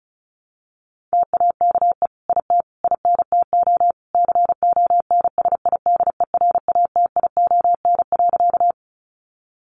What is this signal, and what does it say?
Morse code, heard as a keyed tone, about 34 wpm, 700 Hz. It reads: TAKE IT INTO CONSIDERATION.
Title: Sentences from Top 1500 Words 35wpm